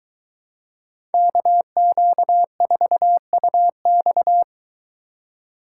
Morse code, heard as a keyed tone, about 22 wpm, 700 Hz. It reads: KQ4UX